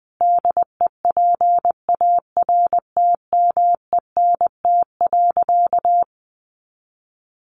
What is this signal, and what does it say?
Morse code, heard as a keyed tone, about 20 wpm, 700 Hz. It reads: DEPARTMENT.
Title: Sentences from Top 1200 Words 20wpm